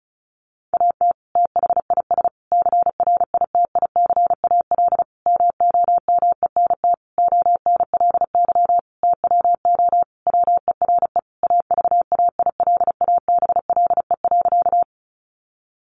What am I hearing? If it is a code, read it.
Morse code, heard as a keyed tone, about 34 wpm, 700 Hz. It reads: AT THIS CRITICAL MOMENT ONLY TWO WERE AVAILABLE.
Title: Sentences from Top 1100 Words 35wpm